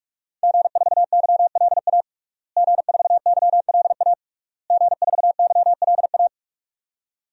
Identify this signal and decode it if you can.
Morse code, heard as a keyed tone, about 46 wpm, 700 Hz. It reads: G4YLA G4YLA G4YLA
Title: DXCC-100 Call Signs - 5 Characters - Repeated 3x - 45wpm